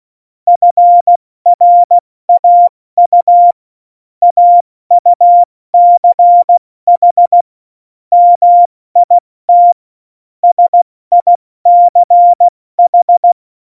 Morse code, heard as a keyed tone, about 16 wpm, 700 Hz. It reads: FRAU AUCH MIT SICH
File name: Morsekurs / Witze / Witz 23. Juli 2019, 70 F80